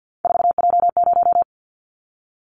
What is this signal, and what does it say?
Morse code, heard as a keyed tone, about 50 wpm, 700 Hz. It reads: 4J0
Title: Callsign Prefixes for Top 100 DXCC - 50wpm